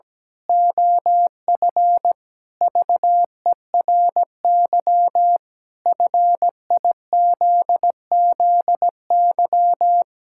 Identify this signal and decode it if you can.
Morse code, heard as a keyed tone, about 17 wpm, 700 Hz. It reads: OF VERY FIZZY
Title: Pangram Challenge v2 - 17wpm